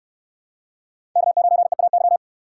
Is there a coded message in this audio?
Yes: Morse code from 1.2 to 2.2 s, heard as a keyed tone, about 66 wpm, 700 Hz. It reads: K9EAY